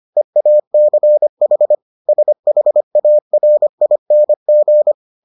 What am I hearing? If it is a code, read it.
Morse code, heard as a keyed tone, about 25 wpm, 600 Hz. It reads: EACH SHARING